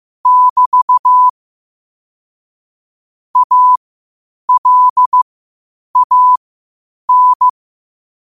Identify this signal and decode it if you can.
Morse code, heard as a keyed tone, about 15 wpm, spaced as for 8 wpm, 1000 Hz. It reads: <BT> ALAN
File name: qotdfiles / 8wpm